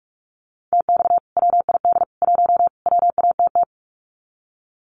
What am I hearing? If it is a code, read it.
Morse code, heard as a keyed tone, about 44 wpm, 700 Hz. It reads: TX WID 1 WATT
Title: Common QSO Phrases 45wpm